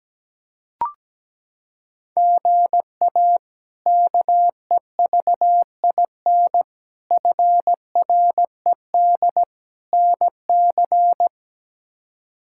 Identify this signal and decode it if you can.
Morse code, heard as a keyed tone, about 17 wpm, 700 Hz. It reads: GA KEVIN FRED NC
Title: K1USN SST QSO Practice - 17wpm